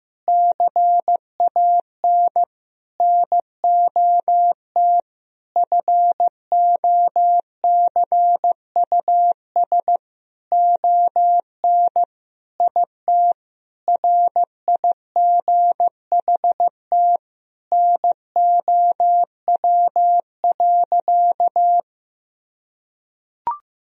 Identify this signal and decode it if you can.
Morse code, heard as a keyed tone, about 15 wpm, 700 Hz. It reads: CAN NOT FOCUS ON IT RIGHT NOW.
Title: Sentences from Top 1000 Words - Review Part 4 - 15wpm